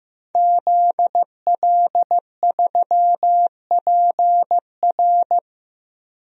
Morse code, heard as a keyed tone, about 15 wpm, 700 Hz. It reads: ZL3PR
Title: DXCC-100 Call Signs - 5 Characters - 15wpm